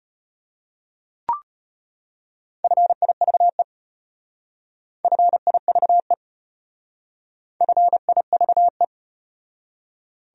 Morse code, heard as a keyed tone, about 33 wpm, 700 Hz. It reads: FIVE FIVE FIVE